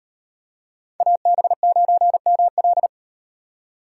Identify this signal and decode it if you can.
Morse code, heard as a keyed tone, about 38 wpm, 700 Hz. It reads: AB9ML